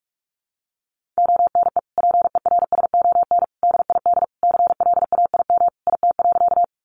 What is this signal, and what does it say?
Morse code, heard as a keyed tone, about 44 wpm, 700 Hz. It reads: ONE PERSON DID CLAIM IT.
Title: Sentences from Top 1500 Words 45wpm